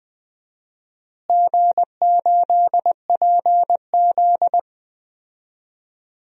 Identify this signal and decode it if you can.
Morse code, heard as a keyed tone, about 20 wpm, 700 Hz. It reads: G8PZ
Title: DXCC-100 Call Signs - 4 Characters - Repeated 2x - 20wpm